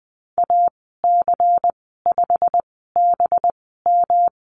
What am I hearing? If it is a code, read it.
Morse code, heard as a keyed tone, about 19 wpm, spaced as for 14 wpm, 700 Hz. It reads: AC5BM
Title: 193 Lesson 31g - Letter v - Call Signs 20wpm